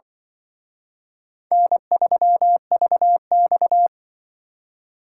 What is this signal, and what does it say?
Morse code, heard as a keyed tone, about 25 wpm, 700 Hz. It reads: N3VX
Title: Speed Racing - USA 1x2 Call Signs 20wpm